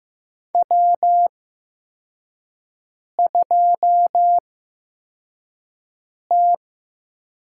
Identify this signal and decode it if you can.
Morse code, heard as a keyed tone, about 15 wpm, spaced as for 4 wpm, 700 Hz. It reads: W2T
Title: USA Call Signs - 8x Character Spacing - 15wpm